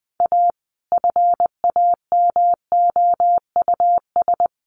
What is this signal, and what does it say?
Morse code, heard as a keyed tone, about 20 wpm, 700 Hz. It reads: A FAMOUS
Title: Sentences from Top 1500 Words 20wpm